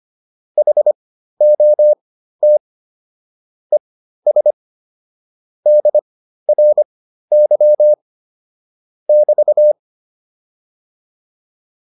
Morse code, heard as a keyed tone, about 25 wpm, spaced as for 13 wpm, 600 Hz. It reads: HOT ES DRY <BT>